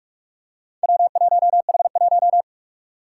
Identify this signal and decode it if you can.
Morse code, heard as a keyed tone, about 46 wpm, 700 Hz. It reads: W1H1
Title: Four Challenging Letter-Number Combinations 45wpm